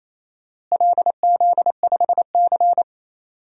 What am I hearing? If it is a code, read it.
Morse code, heard as a keyed tone, about 27 wpm, 700 Hz. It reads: LZ5C
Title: DXCC-100 Call Signs - 4 Characters - Repeated 3x - 28wpm